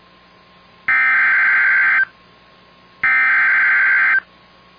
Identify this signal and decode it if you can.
Emergency Alert System (SAME) tones (header bursts), heard as harsh buzzing bursts. It is heard from 0.9 to 4.1 s.